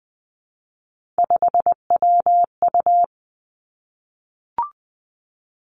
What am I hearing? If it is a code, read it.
Morse code, heard as a keyed tone, about 20 wpm, 700 Hz. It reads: 5WU
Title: Three Challenging Letter-Number Combinations 20wpm